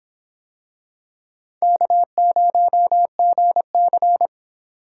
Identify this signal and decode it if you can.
Morse code, heard as a keyed tone, about 26 wpm, 700 Hz. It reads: K0GC